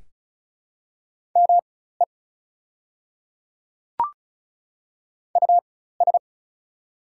Morse code, heard as a keyed tone, about 34 wpm, spaced as for 16 wpm, 700 Hz. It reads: ME US
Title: 4x Character Spacing - Top 200 Words - 35wpm